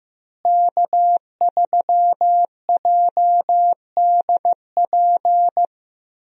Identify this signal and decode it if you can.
Morse code, heard as a keyed tone, about 15 wpm, 700 Hz. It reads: K3JDP
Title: USA Call Signs - 5 Characters - Repeated 2x - 15wpm